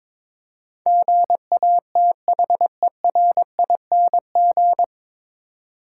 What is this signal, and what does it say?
Morse code, heard as a keyed tone, about 21 wpm, 700 Hz. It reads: GATHERING
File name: 9 Letter Words - First 3 Letters Given - 22wpm